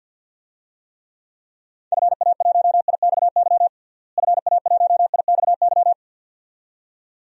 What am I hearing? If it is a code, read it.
Morse code, heard as a keyed tone, about 47 wpm, 700 Hz. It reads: FA1IXY FA1IXY
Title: DXCC-100 Call Signs - 6 Characters - Repeated 2x - 50wpm